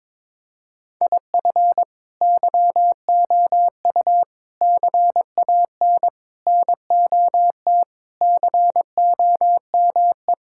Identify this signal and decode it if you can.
Morse code, heard as a keyed tone, about 22 wpm, 700 Hz. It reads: IF YOU CAN NOT COME
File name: practice / 22wpm / Sentences with 4-Word Phrases - 22wpm